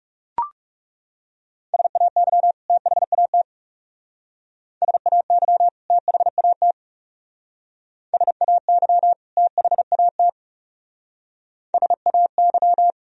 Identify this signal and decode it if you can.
Morse code, heard as a keyed tone, about 35 wpm, 700 Hz. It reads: SAY THAT SAY THAT SAY THAT SAY
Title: Speed-Racing - Sets of 2 Words 30wpm